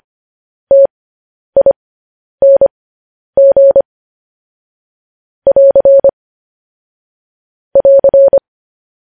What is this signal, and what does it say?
Morse code, heard as a keyed tone, about 25 wpm, spaced as for 10 wpm, 550 Hz. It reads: TING <AR> <AR>
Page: News Headlines in Morse Code at 10 WPM